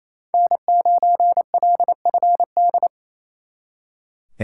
Morse code, heard as a keyed tone, about 35 wpm, 700 Hz. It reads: N9LFD
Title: Speed Racing - USA 1x3 Call Signs 28wpm